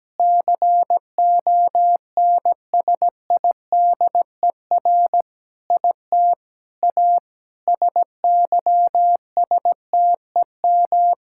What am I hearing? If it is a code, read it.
Morse code, heard as a keyed tone, about 17 wpm, 700 Hz. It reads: CONSIDER IT A SYSTEM
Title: Sentences from Top 1000 Words - Review Part 6 - 17wpm